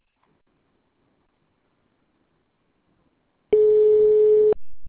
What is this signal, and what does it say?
Ringback tone (Europe, 425 Hz).